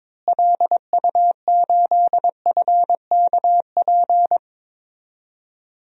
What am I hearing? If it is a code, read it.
Morse code, heard as a keyed tone, about 21 wpm, 700 Hz. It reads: LU8FKP
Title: DXCC-100 Call Signs - 6 Characters - Repeated 3x - 22wpm